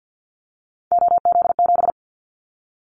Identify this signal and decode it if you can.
Morse code, heard as a keyed tone, about 50 wpm, 700 Hz. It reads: OZ7
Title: Callsign Prefixes for Top 50 DXCC - 50wpm